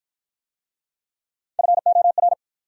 Morse code, heard as a keyed tone, about 53 wpm, 700 Hz. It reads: FOR